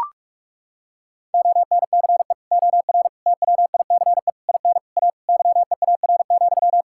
Morse code, heard as a keyed tone, about 41 wpm, 700 Hz. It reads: ONCE OR TWICE IN A YEAR,
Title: Sentences from Top 1600 Words 45wpm